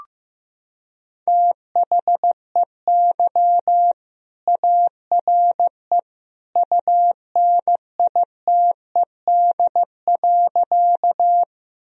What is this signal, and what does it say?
Morse code, heard as a keyed tone, about 15 wpm, 700 Hz. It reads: THEY ARE UNITED.
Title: Sentences from Top 1000 Words - Review Part 4 - 15wpm